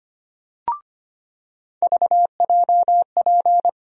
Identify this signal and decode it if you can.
Morse code, heard as a keyed tone, about 25 wpm, 700 Hz. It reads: VJP
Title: USA Call Sign Postfixes - 3 Letters 25wpm